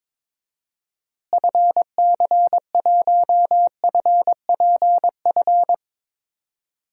Morse code, heard as a keyed tone, about 21 wpm, 700 Hz. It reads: FC1FPF
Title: DXCC-100 Call Signs - 6 Characters - Repeated 3x - 22wpm